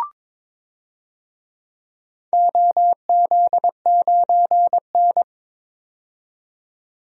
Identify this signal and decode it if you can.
Morse code, heard as a keyed tone, about 22 wpm, 700 Hz. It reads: OZ9N
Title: DXCC-100 Call Signs - 4 Characters - 22wpm